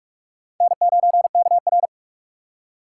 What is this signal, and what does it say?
Morse code, heard as a keyed tone, about 46 wpm, 700 Hz. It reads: N9KR
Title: K1USN SST QSO Practice - 45wpm